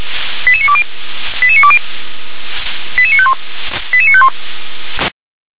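Five-tone selective call (ZVEI), heard as quick runs of five tones, tones of about 70 ms. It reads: 88920 88920 88951 88951
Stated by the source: ZVEI five-tone selective call received on an SDR.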